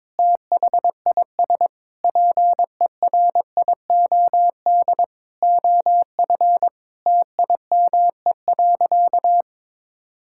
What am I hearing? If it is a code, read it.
Morse code, heard as a keyed tone, about 21 wpm, 700 Hz. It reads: THIS PERIOD OF TIME.
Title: Sentences from Top 1000 Words - Review Part 5 - 22wpm